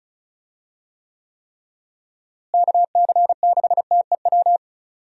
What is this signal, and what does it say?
Morse code, heard as a keyed tone, about 36 wpm, 700 Hz. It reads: KC6TEW
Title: USA Call Signs - 6 Characters - Repeated 3x - 35wpm